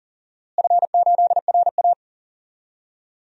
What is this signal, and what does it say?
Morse code, heard as a keyed tone, about 40 wpm, 700 Hz. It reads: F8RA
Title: DXCC-100 Call Signs - 4 Characters - Repeated 3x - 40wpm